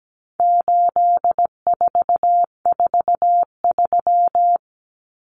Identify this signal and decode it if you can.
Morse code, heard as a keyed tone, about 17 wpm, 700 Hz. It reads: 8443
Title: Common Cut Numbers - 4-Digit 17wpm